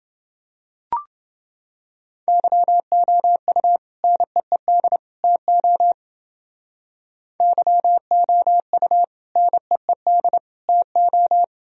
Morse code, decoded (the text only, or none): YOU NEED TO YOU NEED TO